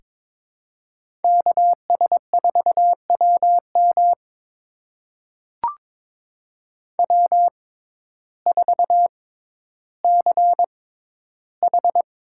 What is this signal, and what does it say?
Morse code, heard as a keyed tone, about 23 wpm, spaced as for 12 wpm, 700 Hz. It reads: KS4WM W4CH
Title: USA Call Signs - 5 Characters - 6x Character Spacing - 22wpm